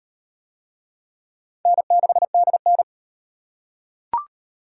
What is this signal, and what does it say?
Morse code, heard as a keyed tone, about 37 wpm, 700 Hz. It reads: N6DN